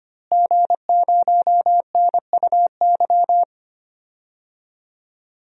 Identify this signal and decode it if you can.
Morse code, heard as a keyed tone, about 25 wpm, 700 Hz. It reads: G0NUY
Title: DXCC-100 Call Signs - 5 Characters - Repeated 2x - 25wpm